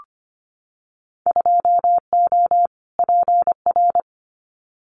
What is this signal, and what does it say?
Morse code, heard as a keyed tone, about 25 wpm, 700 Hz. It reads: 2O PR
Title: Winter Field Day Exchanges Encore 25wpm